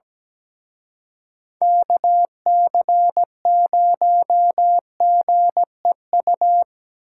Morse code, heard as a keyed tone, about 19 wpm, 700 Hz. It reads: KC0GEU
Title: Speed Racing - USA Call Signs 17wpm